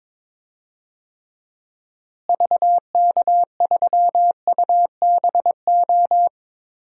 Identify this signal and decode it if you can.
Morse code, heard as a keyed tone, about 22 wpm, 700 Hz. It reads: VK3UBO
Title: DXCC-100 Call Signs - 6 Characters - Repeated 2x - 22wpm